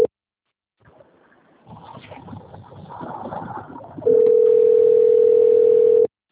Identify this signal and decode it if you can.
Ringback tone (North America), 440 + 480 Hz.
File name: out- - - - - .46419